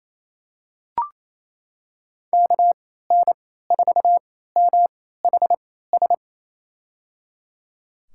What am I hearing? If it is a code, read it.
Morse code, heard as a keyed tone, about 28 wpm, spaced as for 16 wpm, 700 Hz. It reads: KN4MHS
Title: USA Call Signs - 6 Characters - 3x Character Spacing - 28wpm